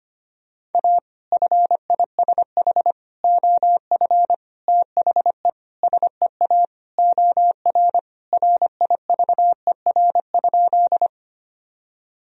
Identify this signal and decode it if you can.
Morse code, heard as a keyed tone, about 25 wpm, 700 Hz. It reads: A FISH OF THE SEA OR RIVER?
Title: Sentences from Top 1000 Words - Review Part 1 - 25wpm